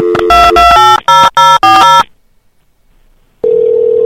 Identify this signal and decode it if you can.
DTMF, heard as telephone touch tones, tones of about 210 ms. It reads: 2280040